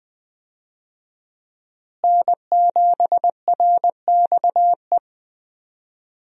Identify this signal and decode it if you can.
Morse code, heard as a keyed tone, about 20 wpm, 700 Hz. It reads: N7RXE
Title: USA Call Signs - 5 Characters - Repeated 2x - 20wpm